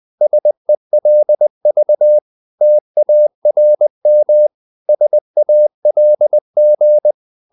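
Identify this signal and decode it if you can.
Morse code, heard as a keyed tone, about 20 wpm, 600 Hz. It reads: SELV TARM SALG
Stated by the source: Morse code practice of short Danish words at 20 wpm, from CW med Gnister.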